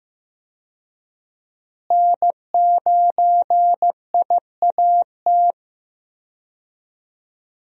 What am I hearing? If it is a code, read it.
Morse code, heard as a keyed tone, about 15 wpm, 700 Hz. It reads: N9IAT
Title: USA Call Signs - 5 Characters - Repeated 2x - 15wpm